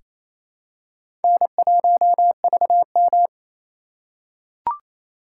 Morse code, heard as a keyed tone, about 27 wpm, 700 Hz. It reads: N1VM
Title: USA Call Signs - 4 Characters - 4x Character Spacing - 28wpm